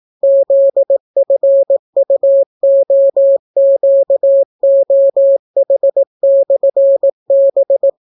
Morse code, heard as a keyed tone, about 18 wpm, 550 Hz. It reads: ZFUOQOH/B